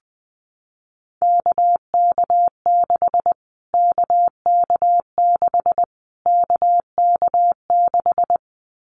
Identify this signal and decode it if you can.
Morse code, heard as a keyed tone, about 20 wpm, 700 Hz. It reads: KK6 KK6 KK6
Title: Callsign Prefixes for Top 25 DXCC - Repeated 3x - 20wpm